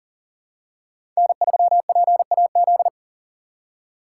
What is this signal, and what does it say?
Morse code, heard as a keyed tone, about 40 wpm, 700 Hz. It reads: N3PAZ
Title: USA Call Signs - 4x Character Spacing - 40wpm